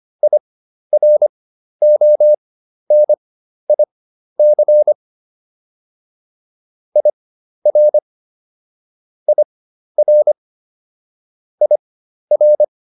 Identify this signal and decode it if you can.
Morse code, heard as a keyed tone, about 25 wpm, spaced as for 12 wpm, 600 Hz. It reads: IRONIC IR IR IR